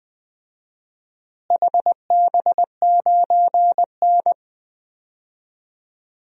Morse code, heard as a keyed tone, about 20 wpm, 700 Hz. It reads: HB9N
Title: DXCC-100 Call Signs - 4 Characters - Repeated 3x - 20wpm